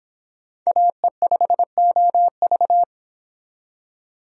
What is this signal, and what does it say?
Morse code, heard as a keyed tone, about 26 wpm, 700 Hz. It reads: AE5OV